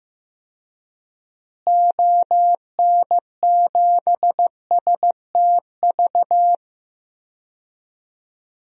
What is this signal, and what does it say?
Morse code, heard as a keyed tone, about 15 wpm, 700 Hz. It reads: ON7STV
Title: DXCC-100 Call Signs - 6 Characters - Repeated 2x - 15wpm